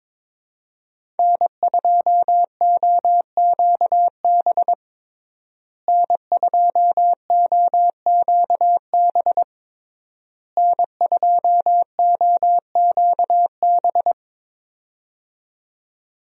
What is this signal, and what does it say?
Morse code, heard as a keyed tone, about 22 wpm, 700 Hz. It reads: N2OQB N2OQB N2OQB
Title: USA Call Signs - 5 Characters - Repeated 3x - 22wpm